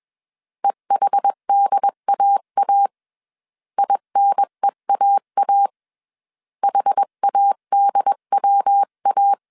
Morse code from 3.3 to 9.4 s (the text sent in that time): INEAA HADWA